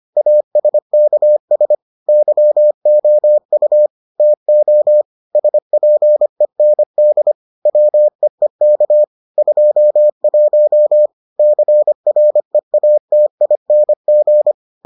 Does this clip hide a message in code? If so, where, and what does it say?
Yes: Morse code from 0.2 to 14.5 s, heard as a keyed tone, about 25 wpm, 600 Hz. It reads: ASKS YOU TO SPEND WEEK 21 CREATING